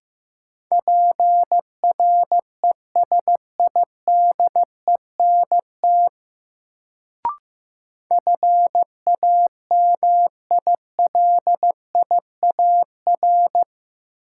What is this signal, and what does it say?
Morse code, heard as a keyed tone, about 15 wpm, 700 Hz. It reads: PRESIDENT FAMILIAR
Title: Sentences from Top 1000 Words - New Word Drill - 15wpm